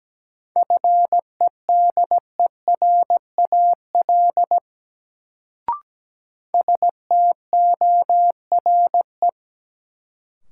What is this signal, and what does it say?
Morse code, heard as a keyed tone, about 17 wpm, 700 Hz. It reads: FEDERAL STORE
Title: Sentences from Top 1000 Words - Review Part 5 - New Word Drill - 17wpm